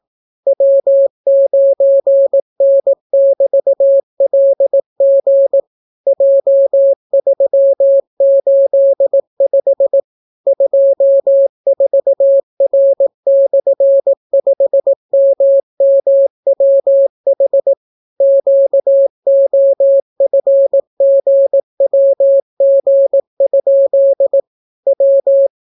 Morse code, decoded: W9N<BT>LG J385 24R/5MMWH QOFGWG? W

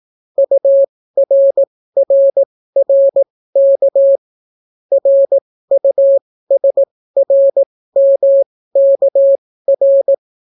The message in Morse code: URRRK RUSRMKR

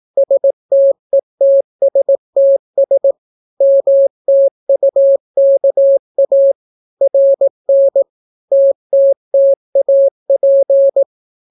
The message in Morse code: STETSTS MTUKA RN TTTAP

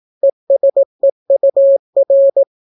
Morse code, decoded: ESEUR

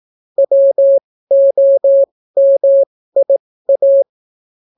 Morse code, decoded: WOMIA

